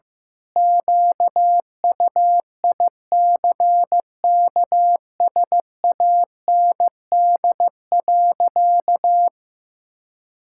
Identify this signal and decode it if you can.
Morse code, heard as a keyed tone, about 15 wpm, 700 Hz. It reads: QUICKSAND.
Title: Pangram Challenge v2 - 15wpm